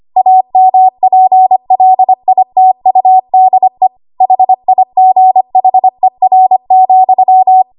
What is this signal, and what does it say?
Morse code, heard as a keyed tone, about 25 wpm, 750 Hz. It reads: AMPLITUDE HIGHER,
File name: Archive / 25 WPM / 25WPM